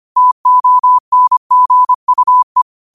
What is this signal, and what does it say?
Morse code, heard as a keyed tone, about 25 wpm, 1000 Hz. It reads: TONGUE
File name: qotdfiles / 25wpm